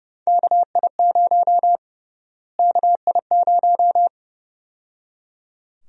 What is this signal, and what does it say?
Morse code, heard as a keyed tone, about 30 wpm, 700 Hz. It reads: KI0 KI0
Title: USA Call Sign Prefixes - Repeated 2x - 30wpm